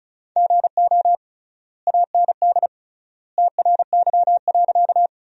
Morse code, heard as a keyed tone, about 36 wpm, 700 Hz. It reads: GO AND TRY.
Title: 3x Word-Spacing - Sentences from Top 200 Words - 35wpm